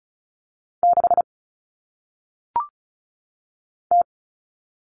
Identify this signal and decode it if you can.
Morse code, heard as a keyed tone, about 36 wpm, 700 Hz. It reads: 6 T
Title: All Cut Numbers - 1-Digit 35wpm